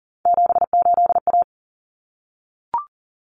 Morse code, heard as a keyed tone, about 40 wpm, 700 Hz. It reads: 78A
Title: Common Cut Numbers - 3-Digit 40wpm